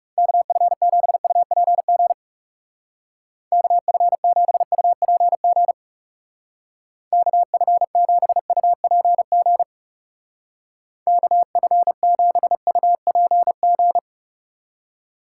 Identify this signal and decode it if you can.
Morse code, heard as a keyed tone, about 37 wpm, 700 Hz. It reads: KF7UPG KF7UPG KF7UPG KF7UPG
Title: Speed Racing - USA 2x3 Call Signs 30wpm